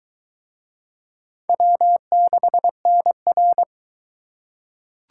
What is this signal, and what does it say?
Morse code, heard as a keyed tone, about 22 wpm, 700 Hz. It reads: W6NR